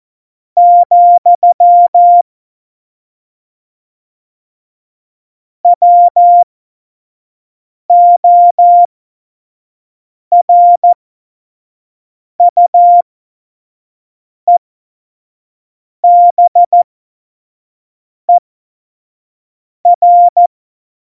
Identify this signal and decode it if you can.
Morse code, heard as a keyed tone, about 14 wpm, spaced as for 5 wpm, 700 Hz. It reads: , WORUEBER
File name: Morsekurs / Witze / Witz 31.Juli 2018, 25 F70